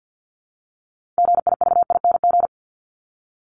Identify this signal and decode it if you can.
Morse code, heard as a keyed tone, about 50 wpm, 700 Hz. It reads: GIVING
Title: ING Suffix - Top 100 Words with Base Words 50wpm